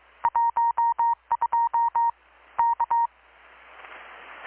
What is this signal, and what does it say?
Morse code, heard as a keyed tone, about 21 wpm, 930 Hz. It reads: 12 K
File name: blog / swl / 6-abril-2019-12695-USB-KPS